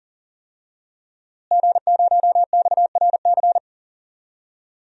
Morse code, heard as a keyed tone, about 40 wpm, 700 Hz. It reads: G0XRC
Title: DXCC-100 Call Signs - 5 Characters - Repeated 2x - 40wpm